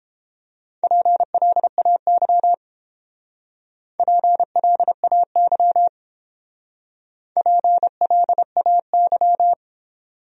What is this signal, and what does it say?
Morse code, heard as a keyed tone, about 29 wpm, 700 Hz. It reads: PLAY PLAY PLAY